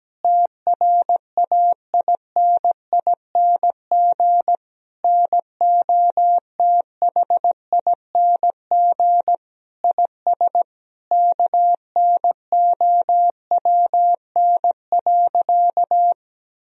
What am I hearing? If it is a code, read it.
Morse code, heard as a keyed tone, about 17 wpm, 700 Hz. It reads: TRAINING NOTHING IS KNOWN.
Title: Sentences from Top 1000 Words - Review Part 4 - 17wpm